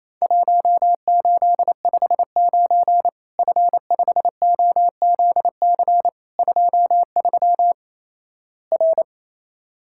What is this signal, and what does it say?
Morse code, heard as a keyed tone, about 28 wpm, 700 Hz. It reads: 1859 F5OZC 23 R
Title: Worked All Europe DX Contest - QTC Exchange - 28wpm